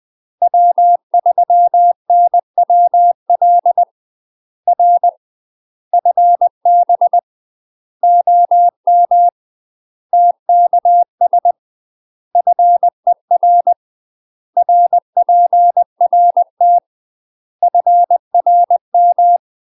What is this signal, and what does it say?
Morse code, heard as a keyed tone, about 20 wpm, 700 Hz. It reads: W3NWL R FB OM TKS FER RPRT FRM
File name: UK MASTER QSOs TRAINING / 20WPM / UK MASTER QSOs 0063